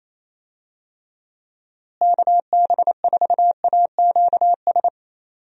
Morse code, heard as a keyed tone, about 28 wpm, 700 Hz. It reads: KB4AQS
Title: USA Call Signs - 6 Characters - Repeated 2x - 28wpm